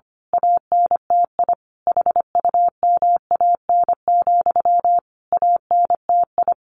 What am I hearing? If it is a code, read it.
Morse code, heard as a keyed tone, about 25 wpm, 700 Hz. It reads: ANTI HUMAN, ANTI
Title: A Sound Of Thunder 25wpm